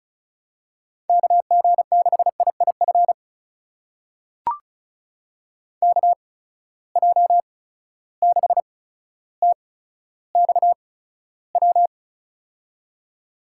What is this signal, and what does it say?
Morse code, heard as a keyed tone, about 36 wpm, spaced as for 15 wpm, 700 Hz. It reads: KG6IIF KJ6TXW
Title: USA Call Signs - 6 Characters - 8x Character Spacing - 35wpm